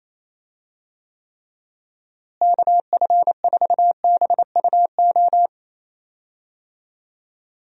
Morse code, heard as a keyed tone, about 28 wpm, 700 Hz. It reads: KF4BUO
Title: USA Call Signs - 6 Characters - Repeated 3x - 28wpm